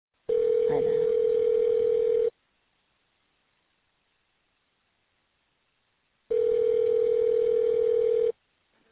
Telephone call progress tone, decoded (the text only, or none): ringback tone